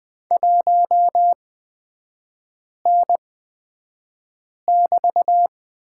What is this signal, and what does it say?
Morse code, heard as a keyed tone, about 20 wpm, spaced as for 5 wpm, 700 Hz. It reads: 1N<BT>
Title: 224 Lesson 36e - BT - Three Characters 20wpm